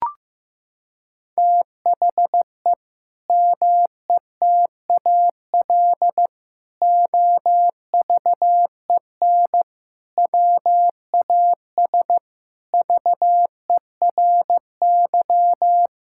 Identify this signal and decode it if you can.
Morse code, heard as a keyed tone, about 15 wpm, 700 Hz. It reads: THE METAL OVEN WAS VERY
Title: Sentences from Top 1000 Words - Review Part 6 - 15wpm